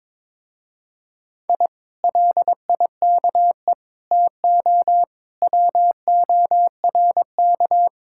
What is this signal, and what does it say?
Morse code, heard as a keyed tone, about 22 wpm, 700 Hz. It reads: I LIKE TO WORK
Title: Speed-Racing Common QSO Phrases Encore - 22wpm